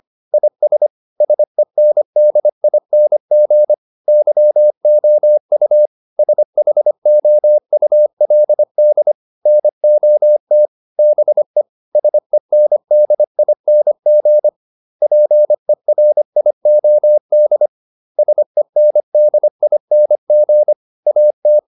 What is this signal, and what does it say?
Morse code, heard as a keyed tone, about 25 wpm, 600 Hz. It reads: IS SENDING YOU SHOULD NOT BE SENDING PERIOD SENDING AT